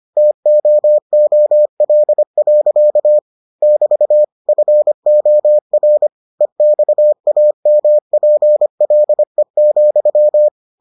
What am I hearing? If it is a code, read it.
Morse code, heard as a keyed tone, about 25 wpm, 600 Hz. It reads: TOOL. <BT> FOR EXAMPLE,